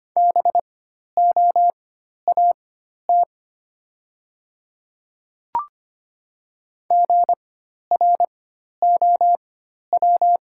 Morse code, heard as a keyed tone, about 25 wpm, spaced as for 12 wpm, 690 Hz. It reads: BOAT GROW
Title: 4x Character Spacing - Top 500 Words - 25wpm